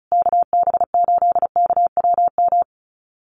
Morse code, heard as a keyed tone, about 34 wpm, 700 Hz. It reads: KB8KWM
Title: USA Call Signs Repeated 3x 35wpm